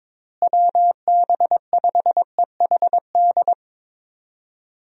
Morse code, heard as a keyed tone, about 22 wpm, 700 Hz. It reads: WB5EHD
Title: USA Call Signs - 6 Characters - Repeated 2x - 22wpm